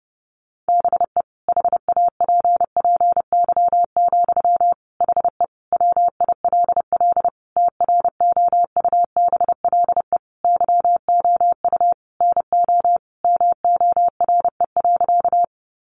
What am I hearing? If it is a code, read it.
Morse code, heard as a keyed tone, about 30 wpm, 700 Hz. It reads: BE HAPPY, HE WILL TROUBLE YOU NO MORE.
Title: Sentences from Top 1200 Words 30wpm